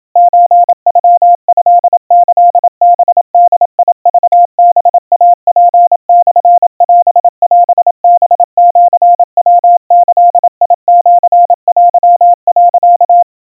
Morse code (Japanese Wabun code), heard as a keyed tone, about 27 wpm, 700 Hz. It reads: ソノトキハボクハイツモオオハシヤギシテ、
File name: 100 cpm 700 Hz Crispy